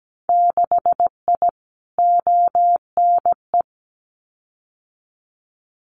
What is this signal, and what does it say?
Morse code, heard as a keyed tone, about 17 wpm, 700 Hz. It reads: 6I ONE
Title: Winter Field Day Exchanges Encore 17wpm